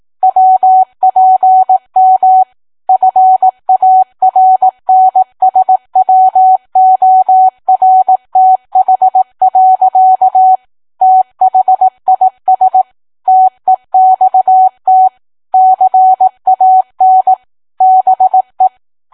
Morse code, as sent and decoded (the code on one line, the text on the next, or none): .-- .--. -- / ..-. .- .-. -. ... .-- --- .-. - .... .-.-.- / - .... .. ... / - . -..- - / -.-. .- -. / -... .
WPM FARNSWORTH. THIS TEXT CAN BE